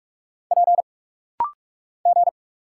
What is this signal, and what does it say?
Morse code, heard as a keyed tone, about 46 wpm, 700 Hz. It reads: P G
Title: Single Letters - Rapid-Fire 45wpm